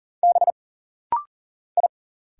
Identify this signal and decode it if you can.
Morse code, heard as a keyed tone, about 40 wpm, 700 Hz. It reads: B I
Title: Single Letter-Number - Rapid-Fire 40wpm